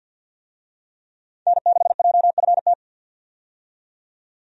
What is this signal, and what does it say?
Morse code, heard as a keyed tone, about 48 wpm, 700 Hz. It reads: N6JFT
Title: USA Call Signs - 5 Characters - 50wpm